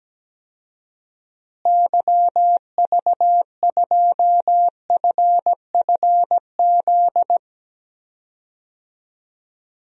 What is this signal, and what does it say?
Morse code, heard as a keyed tone, about 17 wpm, 700 Hz. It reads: YV2FFZ
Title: DXCC-100 Call Signs - 6 Characters - Repeated 3x - 17wpm